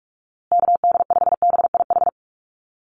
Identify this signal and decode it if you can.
Morse code, heard as a keyed tone, about 44 wpm, 690 Hz. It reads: KD5BIH
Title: USA Call Signs Repeated 3x 45wpm